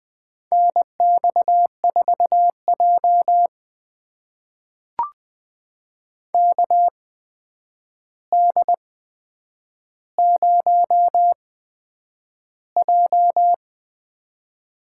Morse code, heard as a keyed tone, about 20 wpm, spaced as for 9 wpm, 700 Hz. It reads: NX4J KD0J
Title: USA Call Signs - 8x Character Spacing - 20wpm